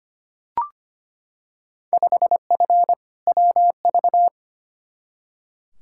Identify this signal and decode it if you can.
Morse code, heard as a keyed tone, about 25 wpm, 700 Hz. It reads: 5F WV
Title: Field Day Exchanges Encore - 25wpm